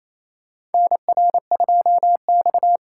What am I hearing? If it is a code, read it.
Morse code, heard as a keyed tone, about 28 wpm, 700 Hz. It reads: NR2X
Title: USA Call Signs - 4 Characters - 4x Character Spacing - 28wpm